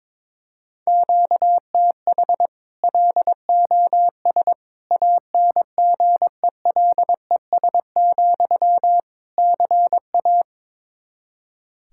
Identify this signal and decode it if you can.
Morse code, heard as a keyed tone, about 22 wpm, 700 Hz. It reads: QTH LOS ANGELES, CA
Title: Practice QSO - Exchange 1 - Beginning - 22wpm